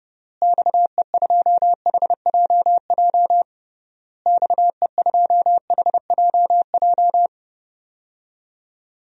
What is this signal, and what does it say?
Morse code, heard as a keyed tone, about 30 wpm, 700 Hz. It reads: XE2HJJ XE2HJJ
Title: DXCC-100 Call Signs - 6 Characters - Repeated 3x - 30wpm